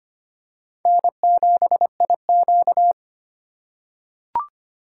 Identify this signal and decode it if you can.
Morse code, heard as a keyed tone, about 25 wpm, 700 Hz. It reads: N7IQ
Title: USA Call Signs - 3x Character Spacing - 25wpm